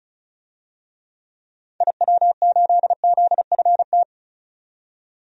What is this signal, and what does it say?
Morse code, heard as a keyed tone, about 34 wpm, 700 Hz. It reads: IW8ZFT
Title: DXCC-100 Call Signs - 6 Characters - 35wpm